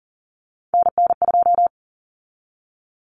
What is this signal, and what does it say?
Morse code, heard as a keyed tone, about 40 wpm, 700 Hz. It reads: NN2
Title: Common Cut Numbers - 3-Digit 40wpm